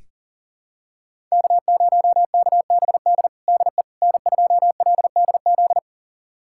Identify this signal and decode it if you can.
Morse code, heard as a keyed tone, about 40 wpm, 700 Hz. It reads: K0KBD DE N2LDZ
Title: USA Callsign DE Callsign - 5 Characters - 40wpm